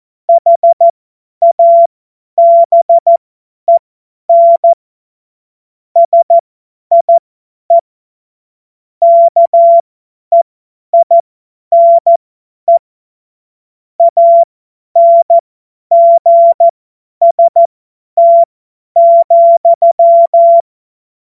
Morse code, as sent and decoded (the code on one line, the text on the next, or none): .... .- -... -. / ... .. / -.- . .. -. / .- -. --. ... - --..--
HABEN SIE KEINE ANGST,